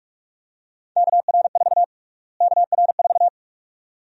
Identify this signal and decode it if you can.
Morse code, heard as a keyed tone, about 46 wpm, 700 Hz. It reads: KR4 KR4
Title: USA Call Sign Prefixes - Repeated 2x - 45wpm